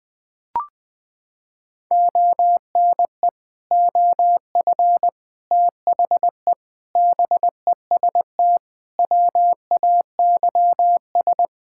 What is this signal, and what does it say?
Morse code, heard as a keyed tone, about 21 wpm, 700 Hz. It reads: ONE OF THE BEST WAYS
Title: Sentences with 6-Word Phrases - 20wpm